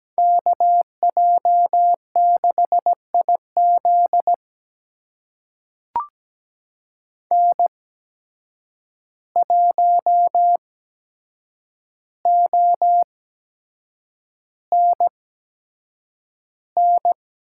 Morse code, heard as a keyed tone, about 17 wpm, spaced as for 7 wpm, 700 Hz. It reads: KJ6IZ N1ONN
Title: USA Call Signs - 5 Characters - 8x Character Spacing - 17wpm